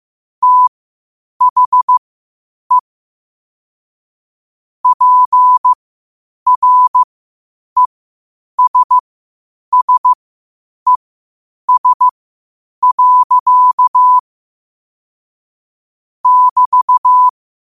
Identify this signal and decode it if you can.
Morse code, heard as a keyed tone, about 15 wpm, spaced as for 8 wpm, 1000 Hz. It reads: THE PRESSES. <BT>
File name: qotdfiles / 8wpm